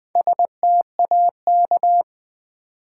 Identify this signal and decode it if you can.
Morse code, heard as a keyed tone, about 20 wpm, 700 Hz. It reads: STAK
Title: Build A Word - 1 Letter At A Time - 5-Letter Words - 20wpm